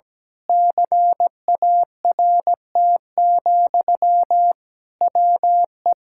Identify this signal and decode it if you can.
Morse code, heard as a keyed tone, about 17 wpm, 700 Hz. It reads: CART, WE
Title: Pangram Challenge v2 - 17wpm